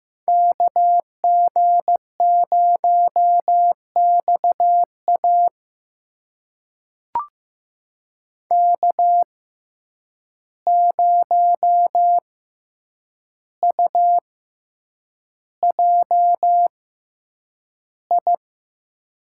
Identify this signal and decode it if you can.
Morse code, heard as a keyed tone, about 15 wpm, spaced as for 8 wpm, 700 Hz. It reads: KG0XA K0UJI
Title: USA Call Signs - 5 Characters - 6x Character Spacing - 15wpm